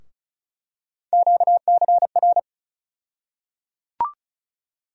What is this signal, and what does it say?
Morse code, heard as a keyed tone, about 36 wpm, 700 Hz. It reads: QCR
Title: USA Call Sign Postfixes - 3 Letters 35wpm